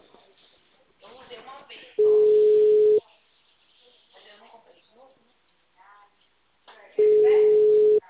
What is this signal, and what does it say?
Ringback tone (Europe, 425 Hz), with speech.
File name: out- -2004- - - .14089